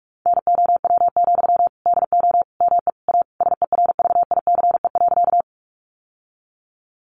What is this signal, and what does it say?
Morse code, heard as a keyed tone, about 44 wpm, 700 Hz. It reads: NOW, DO ME A SERVICE.
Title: Sentences from Top 700 Words 45wpm